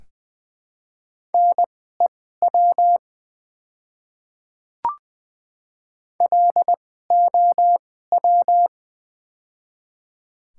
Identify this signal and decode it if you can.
Morse code, heard as a keyed tone, about 20 wpm, spaced as for 14 wpm, 700 Hz. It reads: NEW LOW
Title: 2x Character Spacing - Top 500 Words - 20wpm